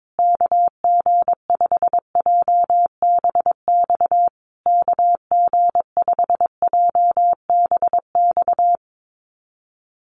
Morse code, heard as a keyed tone, about 22 wpm, 700 Hz. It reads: KG5JBX KG5JBX
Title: USA Call Signs Repeated 3x 22wpm